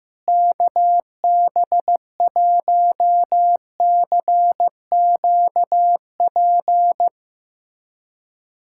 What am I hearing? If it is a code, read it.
Morse code, heard as a keyed tone, about 15 wpm, 700 Hz. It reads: KB1CQP
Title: USA Callsign DE Callsign 15wpm